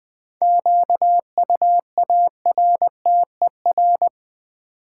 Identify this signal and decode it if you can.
Morse code, heard as a keyed tone, about 20 wpm, 700 Hz. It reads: QUARTER
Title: Sentences from Top 1300 Words - New Word Drill Alt - 20wpm